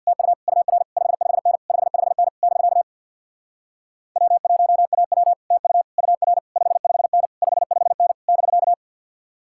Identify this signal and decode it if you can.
Morse code, heard as a keyed tone, about 61 wpm, 700 Hz. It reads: TU UR 55N 55N <BK> W1AW TU UR 55N 55N <BK>
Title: Speed-Racing Common QSO Phrases Encore - 50wpm